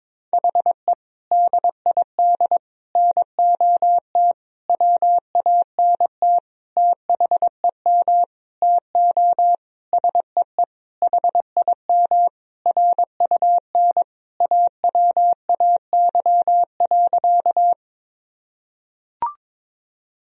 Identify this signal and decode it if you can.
Morse code, heard as a keyed tone, about 22 wpm, 700 Hz. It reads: HE DID NOT WANT THEM TO SEE HIM RUN AWAY.
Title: Sentences from Top 1000 Words - Review Part 1 - 22wpm